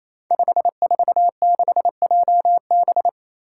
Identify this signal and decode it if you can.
Morse code, heard as a keyed tone, about 27 wpm, 700 Hz. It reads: 546JB
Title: Five Challenging Letter-Number Combinations 28wpm